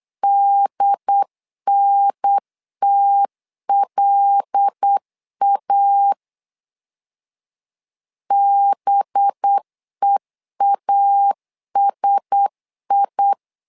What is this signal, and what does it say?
Morse code, heard as a keyed tone, about 8 wpm, 800 Hz. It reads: DNTLA BEASI
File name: LEZIONI / 5 LBVC / 5 LB 8wpm 2816